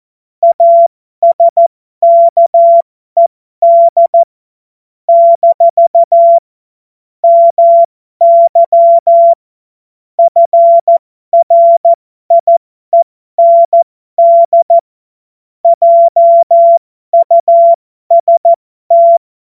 Morse code, heard as a keyed tone, about 14 wpm, spaced as for 12 wpm, 680 Hz. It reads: ASKED - MY FRIEND JUST
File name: Morsekurs / Witze